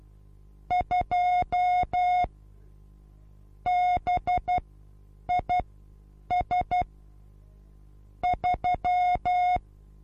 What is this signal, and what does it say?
Morse code, heard as a keyed tone, about 12 wpm, spaced as for 8 wpm, 720 Hz. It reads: 2 BIS 3
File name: Onlinelehrgang / morsen / DJ4UF Morsen23